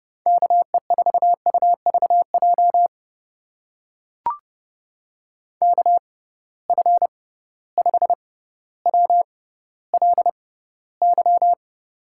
Morse code, heard as a keyed tone, about 30 wpm, spaced as for 15 wpm, 700 Hz. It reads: KE4UVJ KF5WLY
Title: USA Call Signs - 6 Characters - 6x Character Spacing - 30wpm